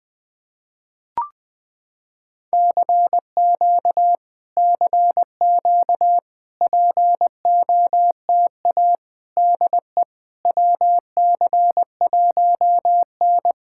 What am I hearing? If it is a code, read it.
Morse code, heard as a keyed tone, about 20 wpm, 700 Hz. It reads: CQ CQ POTA DE WC1N
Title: POTA QSO Practice - 20wpm